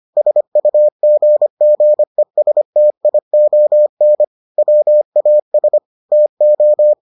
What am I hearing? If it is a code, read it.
Morse code, heard as a keyed tone, about 25 wpm, 600 Hz. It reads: SUGGESTION WAS TO